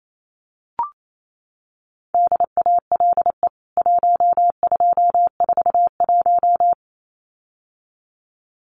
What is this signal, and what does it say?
Morse code, heard as a keyed tone, about 28 wpm, 700 Hz. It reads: DALE 1241
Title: CWT Exchange 28wpm